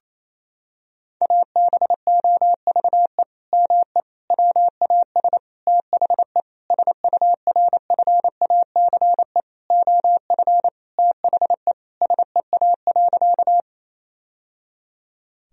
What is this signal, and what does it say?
Morse code, heard as a keyed tone, about 28 wpm, 700 Hz. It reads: ABOVE ME WAS THE SURFACE OF THE SEA.
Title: Sentences from Top 1000 Words - Review Part 3 - 28wpm